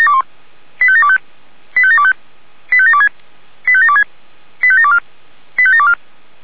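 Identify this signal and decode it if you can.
Five-tone selective call (ZVEI), heard as quick runs of five tones, tones of about 70 ms. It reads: 85725 85725 85726 85726 85724 85724
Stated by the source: Five-tone ZVEI selective call received by SDR.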